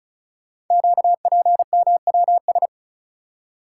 Morse code, heard as a keyed tone, about 36 wpm, 690 Hz. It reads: QPMWS
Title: Code Groups - 5 Letters - 35wpm